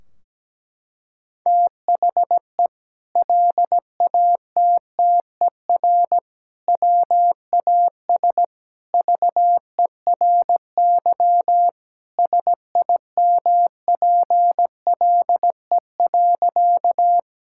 Morse code, heard as a keyed tone, about 17 wpm, 700 Hz. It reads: THE LATTER WAS VERY SIMPLE.
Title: Sentences from Top 1600 Words 17wpm